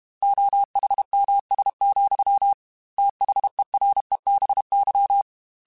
Morse code, heard as a keyed tone, about 32 wpm, 800 Hz. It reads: OHMS, THEREBY
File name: Technical Subjects / Technical Subjects-mp3s / XC @30